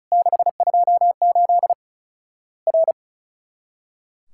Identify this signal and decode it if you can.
Morse code, heard as a keyed tone, about 36 wpm, 700 Hz. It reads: 628 R
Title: Worked All Europe DX Contest - QTC Exchange - 35wpm